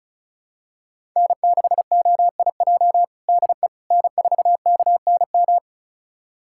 Morse code, heard as a keyed tone, about 36 wpm, 700 Hz. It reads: N6OIJ DE N4KNM
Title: USA Callsign DE Callsign - 5 Characters - 35wpm